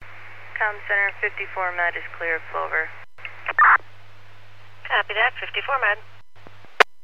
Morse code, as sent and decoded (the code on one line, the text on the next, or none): none